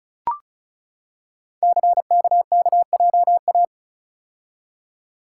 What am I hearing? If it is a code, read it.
Morse code, heard as a keyed tone, about 36 wpm, 700 Hz. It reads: CKKJA
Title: Code Groups - 5 Letters - 35wpm